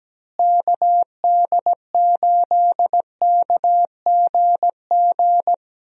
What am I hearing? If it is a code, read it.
Morse code, heard as a keyed tone, about 17 wpm, 700 Hz. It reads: KD8KGG
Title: USA Call Signs - 6 Characters - Repeated 3x - 17wpm